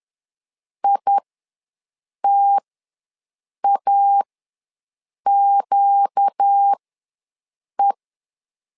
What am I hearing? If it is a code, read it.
Morse code, heard as a keyed tone, about 10 wpm, spaced as for 6 wpm, 800 Hz. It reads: ITAQE
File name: LEZIONI / 4 Q ws 10wpm 2306